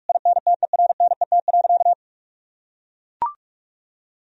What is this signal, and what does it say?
Morse code, heard as a keyed tone, about 41 wpm, 700 Hz. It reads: INTERNET.
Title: Sentences from Top 1000 Words - Review Part 4 - 45wpm